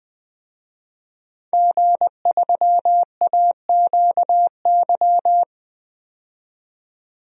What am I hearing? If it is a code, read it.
Morse code, heard as a keyed tone, about 20 wpm, 700 Hz. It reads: G3AQY
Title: DXCC-100 Call Signs - 5 Characters - 20wpm